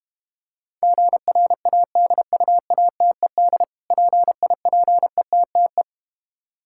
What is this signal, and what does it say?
Morse code, heard as a keyed tone, about 31 wpm, 700 Hz. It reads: GRADUATED PIPETTE
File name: practice / 25wpm / Difficult Words - Speed-Racing with Hints - 25wpm